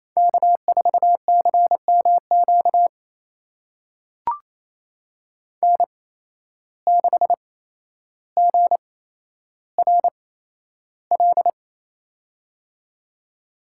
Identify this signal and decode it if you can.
Morse code, heard as a keyed tone, about 28 wpm, spaced as for 12 wpm, 700 Hz. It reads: K4CMQ N6GRL
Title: USA Call Signs - 8x Character Spacing - 28wpm